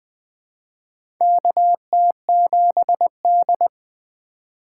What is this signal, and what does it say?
Morse code, heard as a keyed tone, about 20 wpm, 700 Hz. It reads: KT7D
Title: USA Call Signs - 4x Character Spacing - 20wpm